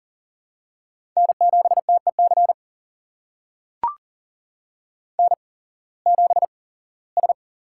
Morse code, heard as a keyed tone, about 40 wpm, spaced as for 21 wpm, 700 Hz. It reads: N7TEC N7S
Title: USA Call Signs - 8x Character Spacing - 40wpm